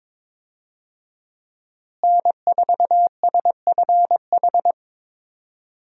Morse code, heard as a keyed tone, about 22 wpm, 700 Hz. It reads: N4SFH
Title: USA Call Signs - 5 Characters - Repeated 2x - 22wpm